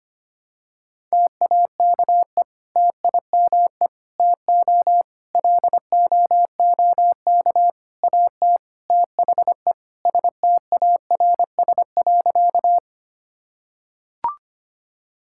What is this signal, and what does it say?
Morse code, heard as a keyed tone, about 25 wpm, 700 Hz. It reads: TAKE TIME TO LOOK AT THE STARS.
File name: practice / 25wpm / Sentences with 4-Word Phrases - 25wpm